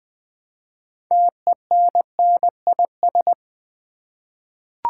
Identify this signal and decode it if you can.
Morse code, heard as a keyed tone, about 20 wpm, 700 Hz. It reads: TENNIS
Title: Six Letter Words - 20wpm